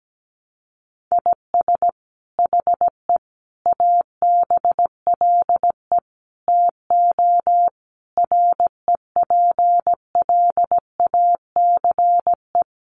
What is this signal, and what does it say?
Morse code, heard as a keyed tone, about 17 wpm, 700 Hz. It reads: IS HE ABLE TO REPLACE
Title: Sentences from Top 1400 Words 17wpm